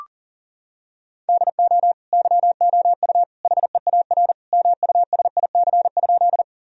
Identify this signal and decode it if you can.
Morse code, heard as a keyed tone, about 40 wpm, 700 Hz. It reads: DO YOU HEAR MUSIC?
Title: Sentences from Top 1000 Words - Review Part 1 - 40wpm